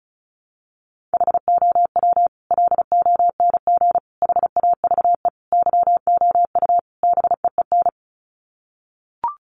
Morse code, heard as a keyed tone, about 34 wpm, 700 Hz. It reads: HOW LONG HAVE YOU BEEN
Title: Sets of 5 Words Encore 35wpm